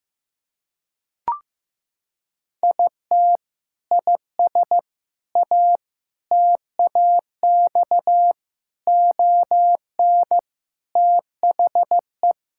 Morse code, heard as a keyed tone, about 15 wpm, 700 Hz. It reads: IT IS A TAX ON THE
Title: Sentences from Top 1000 Words - Review Part 5 - 15wpm